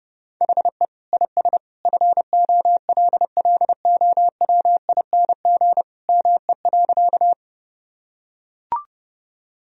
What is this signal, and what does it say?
Morse code, heard as a keyed tone, about 30 wpm, 700 Hz. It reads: HE IS FOLLOWING ME.
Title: Sentences from Top 1000 Words - Review Part 4 - 30wpm